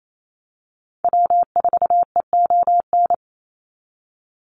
Morse code, heard as a keyed tone, about 28 wpm, 700 Hz. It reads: W4EON
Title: NA SOTA Activator Callsigns 28wpm